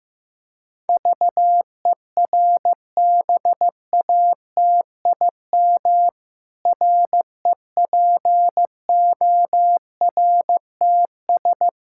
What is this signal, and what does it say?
Morse code, heard as a keyed tone, about 15 wpm, 700 Hz. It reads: VERBATIM REPORTS
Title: Pangram Challenge v2 - 15wpm